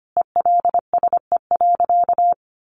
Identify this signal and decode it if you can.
Morse code, heard as a keyed tone, about 25 wpm, 700 Hz. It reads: ELSE.
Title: Sentences from Top 800 Words 25wpm